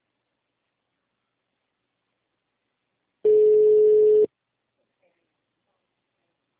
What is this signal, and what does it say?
Ringback tone (Europe, 425 Hz), with speech.
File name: out- -2025- - - .4544